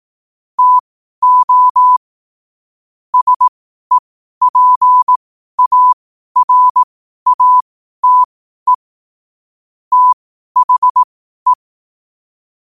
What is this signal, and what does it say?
Morse code, heard as a keyed tone, about 18 wpm, spaced as for 12 wpm, 1000 Hz. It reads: TO SEPARATE THE
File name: qotdfiles / 12wpm